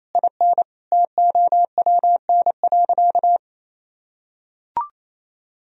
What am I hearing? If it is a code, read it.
Morse code, heard as a keyed tone, about 28 wpm, 700 Hz. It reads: IN TOWN.
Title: Sentences from Top 1000 Words - Review Part 4 - 28wpm